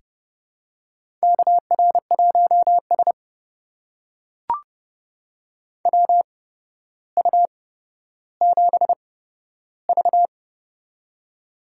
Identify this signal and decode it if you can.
Morse code, heard as a keyed tone, about 31 wpm, spaced as for 13 wpm, 700 Hz. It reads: KR1S WU7V
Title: USA Call Signs - 4 Characters - 8x Character Spacing - 30wpm